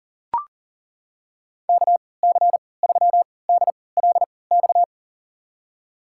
Morse code, heard as a keyed tone, about 39 wpm, spaced as for 22 wpm, 700 Hz. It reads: KC3DLX
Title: USA Call Signs - 6 Characters - 3x Character Spacing - 40wpm